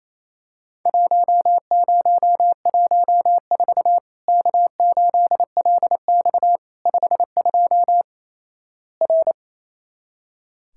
Morse code, heard as a keyed tone, about 28 wpm, 700 Hz. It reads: 1014 K8LX 52 R
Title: Worked All Europe DX Contest - QTC Exchange - 28wpm